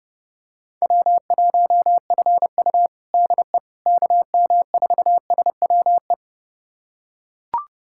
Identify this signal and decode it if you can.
Morse code, heard as a keyed tone, about 30 wpm, 700 Hz. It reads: W1FU DE KM4SWE
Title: USA Callsign DE Callsign 30wpm